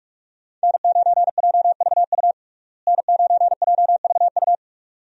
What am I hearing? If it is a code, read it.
Morse code, heard as a keyed tone, about 41 wpm, 700 Hz. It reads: N9JVU N9JVU
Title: USA Call Signs - 5 Characters - Repeated 2x - 45wpm